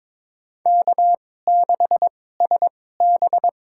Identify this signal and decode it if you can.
Morse code, heard as a keyed tone, about 21 wpm, spaced as for 16 wpm, 700 Hz. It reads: K6SB
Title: USA Call Signs - 4 Characters - 2x Character Spacing - 22wpm